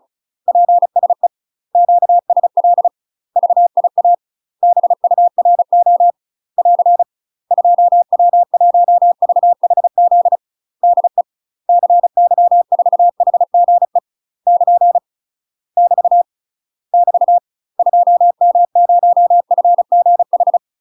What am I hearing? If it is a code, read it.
Morse code, heard as a keyed tone, about 34 wpm, 700 Hz. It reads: PSE QSL VIA BURO <AR> 2W1VHZ DE CY4HGE <KN> <BT> <BT> 2M0FGH
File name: UK MASTER QSOs TRAINING / 35WPM / UK MASTER QSOs 0039